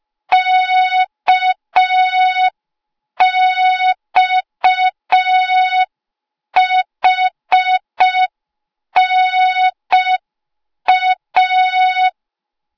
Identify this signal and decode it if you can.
Morse code, heard as a keyed tone, about 5 wpm, 3700 Hz. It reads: KXHNA